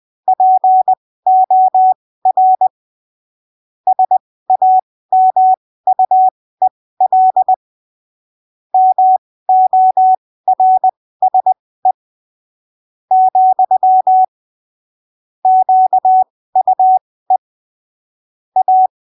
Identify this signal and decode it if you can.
Morse code, heard as a keyed tone, about 20 wpm, spaced as for 15 wpm, 750 Hz. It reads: POR SAMUEL MORSE , QUE A
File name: arquivos / texto / texto 17 20ppm